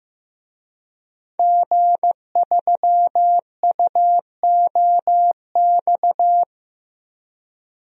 Morse code, heard as a keyed tone, about 15 wpm, 700 Hz. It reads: G3UOX
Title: DXCC-100 Call Signs - 5 Characters - Repeated 2x - 15wpm